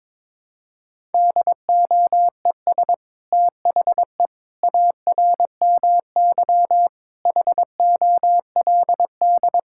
Morse code, heard as a keyed tone, about 22 wpm, 700 Hz. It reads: DOES THE ARMY HOLD
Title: Sentences from Top 1000 Words - Review Part 6 - 22wpm